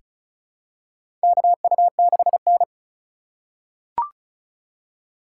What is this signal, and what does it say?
Morse code, heard as a keyed tone, about 36 wpm, 700 Hz. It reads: KU6N
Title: USA Call Signs - 4 Characters - 3x Character Spacing - 35wpm